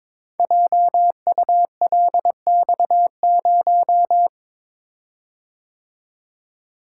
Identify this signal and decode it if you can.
Morse code, heard as a keyed tone, about 22 wpm, 700 Hz. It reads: JULX0
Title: Code Groups - 5 Letter:Numbers - 22wpm